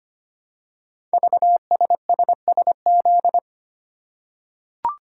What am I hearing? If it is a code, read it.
Morse code, heard as a keyed tone, about 25 wpm, 700 Hz. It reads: VSSSZ
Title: Code Groups - 5 Letters - 25wpm